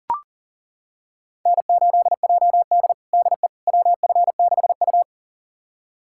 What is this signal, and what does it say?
Morse code, heard as a keyed tone, about 37 wpm, 700 Hz. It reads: N8JD DE WF6U
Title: USA Callsign DE Callsign - 4 Characters - 40wpm